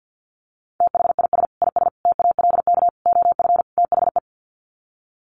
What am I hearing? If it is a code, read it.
Morse code, heard as a keyed tone, about 50 wpm, 700 Hz. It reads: THIS IS TALK OF THE
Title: Sets of 5 Words 50wpm